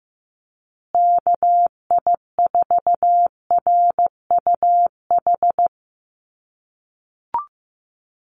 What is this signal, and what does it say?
Morse code, heard as a keyed tone, about 15 wpm, 700 Hz. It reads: KI4RUH
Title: USA Call Signs Repeated 2x 15wpm